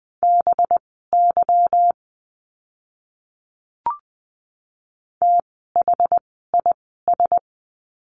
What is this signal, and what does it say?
Morse code, heard as a keyed tone, about 20 wpm, spaced as for 14 wpm, 700 Hz. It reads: BY THIS
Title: 185 Lesson 30f - Letter b - Words 20wpm